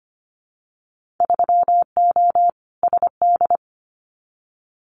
Morse code, heard as a keyed tone, about 25 wpm, 690 Hz. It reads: 3O SD
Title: Winter Field Day Exchanges 25wpm